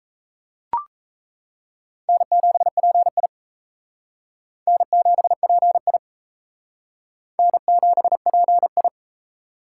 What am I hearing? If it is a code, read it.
Morse code, heard as a keyed tone, about 35 wpm, 700 Hz. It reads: N7PI N7PI N7PI
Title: Speed Racing - USA 1x2 Call Signs 28wpm